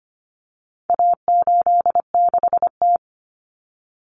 Morse code, heard as a keyed tone, about 25 wpm, 700 Hz. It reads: A86T
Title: Common Cut Numbers - 4-Digit 25wpm